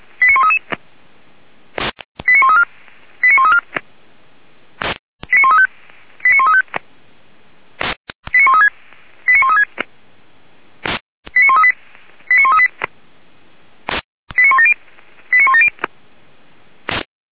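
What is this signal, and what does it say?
Five-tone selective call (ZVEI), heard as quick runs of five tones, tones of about 80 ms. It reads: 89133 89134 89134 89135 89135 89136 89136 89138 89138 89170 89170
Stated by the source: ZVEI five-tone selective call 89132 received on SDR, feuerwehr-orlamuende (fire brigade Orlamünde) archive.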